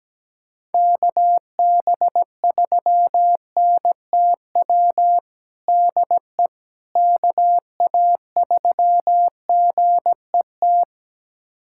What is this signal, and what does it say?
Morse code, heard as a keyed tone, about 17 wpm, 700 Hz. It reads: KB3NTW DE KA3GET
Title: USA Callsign DE Callsign - 6 Characters - 17wpm